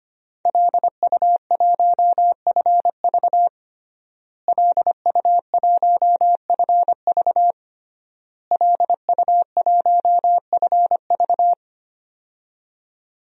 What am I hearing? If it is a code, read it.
Morse code, heard as a keyed tone, about 25 wpm, 700 Hz. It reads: LU1FV LU1FV LU1FV
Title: DXCC-100 Call Signs - 5 Characters - Repeated 3x - 25wpm